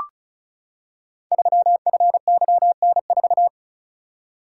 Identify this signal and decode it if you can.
Morse code, heard as a keyed tone, about 36 wpm, 690 Hz. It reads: 3FYN4
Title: Code Groups - 5 Letter:Numbers - 35wpm